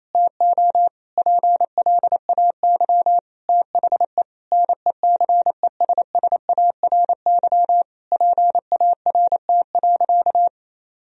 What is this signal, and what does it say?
Morse code, heard as a keyed tone, about 28 wpm, 700 Hz. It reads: TO PLAY THE NECESSARY PART.
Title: Sentences from Top 1000 Words - Review Part 1 - 28wpm